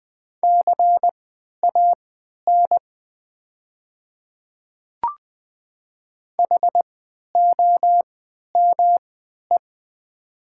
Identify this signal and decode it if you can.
Morse code, heard as a keyed tone, about 20 wpm, spaced as for 11 wpm, 700 Hz. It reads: CAN HOME
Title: 3x Character Spacing - Top 200 Words - 20wpm